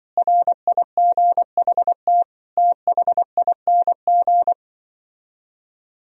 Morse code, heard as a keyed tone, about 24 wpm, 700 Hz. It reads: RIGHT THING
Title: Speed-Racing - Sets of 5 Words 20wpm